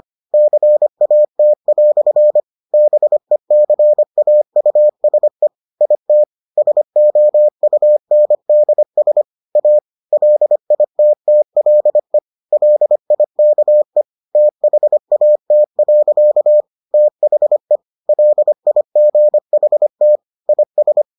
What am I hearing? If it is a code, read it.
Morse code, heard as a keyed tone, about 25 wpm, 600 Hz. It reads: CAT" BECAUSE IT SOUNDS A LITTLE LIKE THAT. THE LIGHT IS